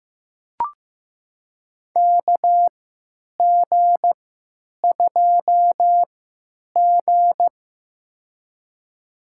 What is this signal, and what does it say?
Morse code, heard as a keyed tone, about 15 wpm, spaced as for 9 wpm, 700 Hz. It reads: KG2G
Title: USA Call Signs - 4 Characters - 3x Character Spacing - 15wpm